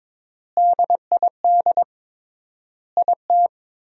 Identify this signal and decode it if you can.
Morse code, heard as a keyed tone, about 22 wpm, 700 Hz. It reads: DID IT
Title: 3x Word-Spacing - Sentences from Top 300 Words - 22wpm